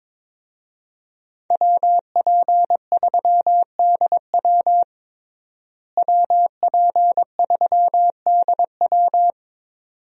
Morse code, heard as a keyed tone, about 22 wpm, 700 Hz. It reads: WP3DW WP3DW
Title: USA Call Signs - 5 Characters - Repeated 2x - 22wpm